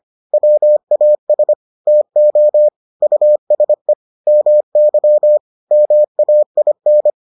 Morse code, heard as a keyed tone, about 25 wpm, 600 Hz. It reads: WAS TO USE MY MAIN